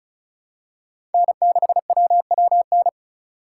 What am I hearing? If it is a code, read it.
Morse code, heard as a keyed tone, about 36 wpm, 700 Hz. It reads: N6WWN
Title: USA Call Signs - 5 Characters - 4x Character Spacing - 35wpm